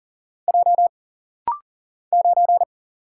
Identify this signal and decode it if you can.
Morse code, heard as a keyed tone, about 40 wpm, 700 Hz. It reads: J 9
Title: Single Letter-Number - Rapid-Fire 40wpm